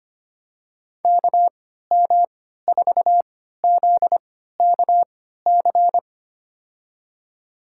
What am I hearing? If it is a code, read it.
Morse code, heard as a keyed tone, about 25 wpm, spaced as for 14 wpm, 700 Hz. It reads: KM4ZKC
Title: USA Call Signs - 6 Characters - 3x Character Spacing - 25wpm